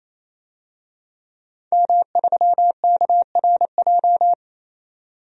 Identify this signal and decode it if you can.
Morse code, heard as a keyed tone, about 27 wpm, 700 Hz. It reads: M3KRJ
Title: DXCC-100 Call Signs - 5 Characters - 28wpm